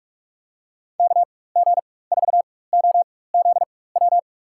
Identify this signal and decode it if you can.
Morse code, heard as a keyed tone, about 47 wpm, spaced as for 21 wpm, 700 Hz. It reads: KG4OZW
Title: USA Call Signs - 6 Characters - 4x Character Spacing - 45wpm